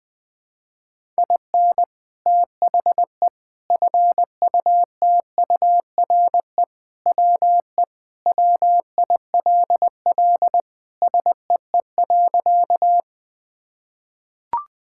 Morse code, heard as a keyed tone, about 21 wpm, 700 Hz. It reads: IN THE FUTURE WE WILL SEE.
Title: Sentences with 3-Word Phrases - 20wpm